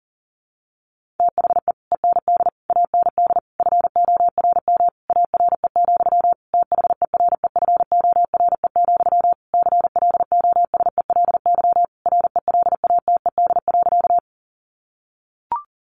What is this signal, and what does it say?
Morse code, heard as a keyed tone, about 40 wpm, 700 Hz. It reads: THE END AND FORM ARE, THEREFORE, CLOSELY RELATED.
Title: Sentences from Top 1400 Words 40wpm